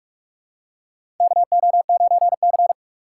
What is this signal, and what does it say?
Morse code, heard as a keyed tone, about 42 wpm, 700 Hz. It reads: KO9C
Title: USA Call Signs - 4 Characters - 4x Character Spacing - 45wpm